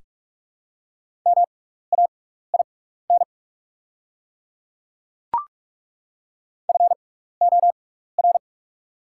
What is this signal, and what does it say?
Morse code, heard as a keyed tone, about 48 wpm, spaced as for 16 wpm, 690 Hz. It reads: MAIN FOR
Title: 6x Character Spacing - Top 300 Words - 45wpm